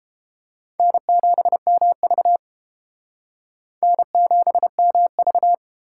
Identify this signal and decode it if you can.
Morse code, heard as a keyed tone, about 32 wpm, 700 Hz. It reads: N7MV N7MV